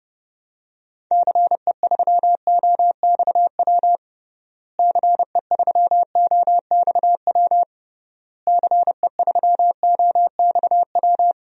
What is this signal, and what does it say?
Morse code, heard as a keyed tone, about 30 wpm, 700 Hz. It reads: CE3OXW CE3OXW CE3OXW
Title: DXCC-100 Call Signs - 6 Characters - Repeated 3x - 30wpm